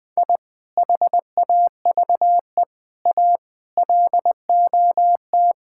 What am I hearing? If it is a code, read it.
Morse code, heard as a keyed tone, about 20 wpm, 700 Hz. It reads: I HAVE A LOT
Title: Sentences with 5-Word Phrases - 20wpm